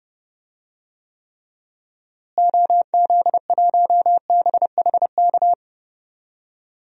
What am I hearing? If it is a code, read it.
Morse code, heard as a keyed tone, about 31 wpm, 700 Hz. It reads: OZ1BHK
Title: DXCC-100 Call Signs - 6 Characters - Repeated 3x - 30wpm